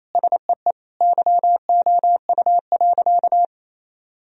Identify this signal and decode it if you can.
Morse code, heard as a keyed tone, about 28 wpm, 700 Hz. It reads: SEE YOU.
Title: Sentences from Top 1000 Words - Review Part 2 - 28wpm